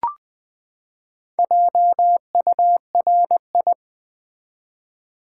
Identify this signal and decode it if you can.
Morse code, heard as a keyed tone, about 20 wpm, 700 Hz. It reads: JURI
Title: Build A Word - 4 Letters At A Time - 12-Letter Words - 20wpm